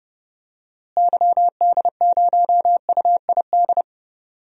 Morse code, heard as a keyed tone, about 30 wpm, 700 Hz. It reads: YD0UID
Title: DXCC-100 Call Signs - 6 Characters - Repeated 3x - 30wpm